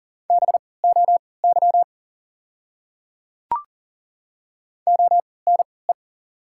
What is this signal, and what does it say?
Morse code, heard as a keyed tone, about 40 wpm, spaced as for 23 wpm, 690 Hz. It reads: BOY ONE
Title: 3x Character Spacing - Top 200 Words - 40wpm